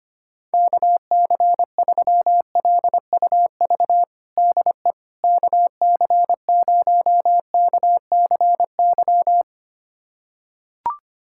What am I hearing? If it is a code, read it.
Morse code, heard as a keyed tone, about 25 wpm, 700 Hz. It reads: KC3LUV DE KC0KCY
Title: USA Callsign DE Callsign 25wpm